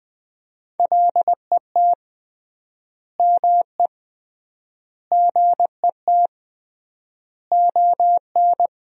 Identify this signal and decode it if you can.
Morse code, heard as a keyed tone, about 21 wpm, 700 Hz. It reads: LET ME GET ON
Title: 3x Word-Spacing - Sentences from Top 300 Words - 20wpm